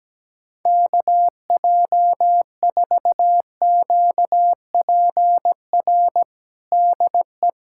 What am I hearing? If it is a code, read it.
Morse code, heard as a keyed tone, about 17 wpm, 700 Hz. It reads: KJ4QPR DE
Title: USA Callsign DE Callsign 17wpm